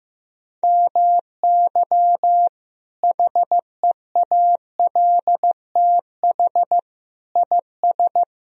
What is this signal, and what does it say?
Morse code, heard as a keyed tone, about 15 wpm, 700 Hz. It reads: MY HEALTH IS
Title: Sentences from Top 1000 Words - Review Part 3 - 15wpm